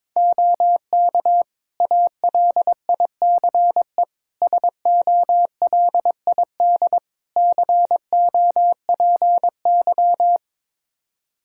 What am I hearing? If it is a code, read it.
Morse code, heard as a keyed tone, about 22 wpm, 700 Hz. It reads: OK ALICE SOLID COPY
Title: Practice QSO - Exchange 3 - More Info - 22wpm